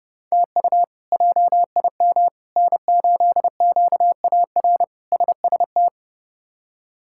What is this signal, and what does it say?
Morse code, heard as a keyed tone, about 30 wpm, 700 Hz. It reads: TU JIM N8QAR SST
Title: K1USN SST QSO Practice - 30wpm